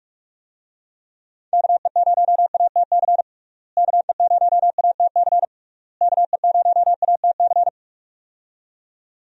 Morse code, heard as a keyed tone, about 45 wpm, 700 Hz. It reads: KE0ATC KE0ATC KE0ATC
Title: USA Call Signs - 6 Characters - Repeated 3x - 45wpm